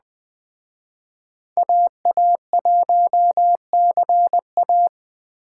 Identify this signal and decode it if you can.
Morse code, heard as a keyed tone, about 20 wpm, 700 Hz. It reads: AA1CA